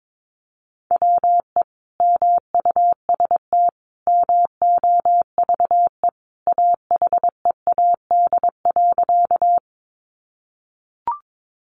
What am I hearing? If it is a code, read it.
Morse code, heard as a keyed tone, about 22 wpm, 700 Hz. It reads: WE MUST MOVE AHEAD.
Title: Sentences from Top 1100 Words 22wpm